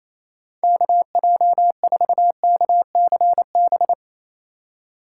Morse code, heard as a keyed tone, about 28 wpm, 700 Hz. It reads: KJ4KCB
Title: USA Call Signs - 6 Characters - 3x Character Spacing - 28wpm